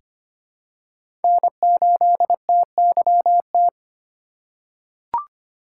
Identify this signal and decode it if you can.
Morse code, heard as a keyed tone, about 25 wpm, 700 Hz. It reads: N8TYT
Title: USA Call Signs - 5 Characters - 2x Character Spacing - 25wpm